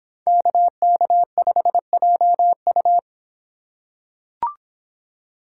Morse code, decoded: KK5JU

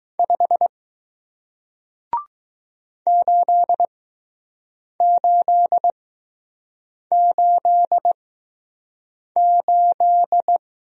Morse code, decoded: 5 8888

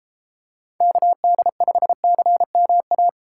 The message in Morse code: KD5CMA

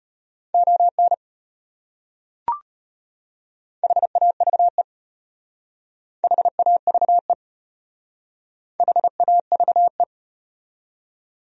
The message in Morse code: ON HAVE HAVE HAVE

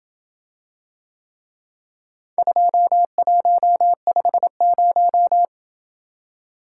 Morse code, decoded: 2150

